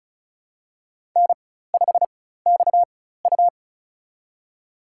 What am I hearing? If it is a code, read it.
Morse code, heard as a keyed tone, about 36 wpm, spaced as for 16 wpm, 700 Hz. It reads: N5XU
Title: USA Call Signs - 4 Characters - 4x Character Spacing - 35wpm